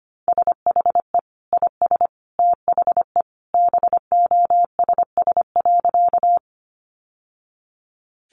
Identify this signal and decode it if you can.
Morse code, heard as a keyed tone, about 25 wpm, 700 Hz. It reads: SHE IS THE BOSS.
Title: Sentences from Top 700 Words 25wpm